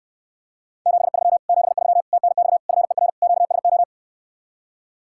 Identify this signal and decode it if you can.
Morse code, heard as a keyed tone, about 69 wpm, 700 Hz. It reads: 73 73 TNX FER QSO